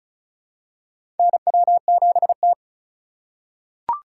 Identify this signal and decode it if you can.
Morse code, heard as a keyed tone, about 36 wpm, 700 Hz. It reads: NW7T
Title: USA Call Signs - 8x Character Spacing - 35wpm